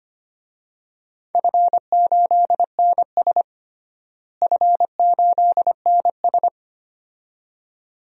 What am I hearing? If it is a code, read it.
Morse code, heard as a keyed tone, about 25 wpm, 700 Hz. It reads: F8NS F8NS
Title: DXCC-100 Call Signs - 4 Characters - Repeated 2x - 25wpm